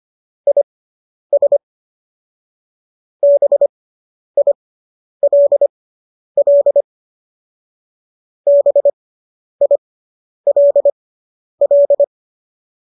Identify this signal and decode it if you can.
Morse code, heard as a keyed tone, about 25 wpm, spaced as for 10 wpm, 570 Hz. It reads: IS BILL BILL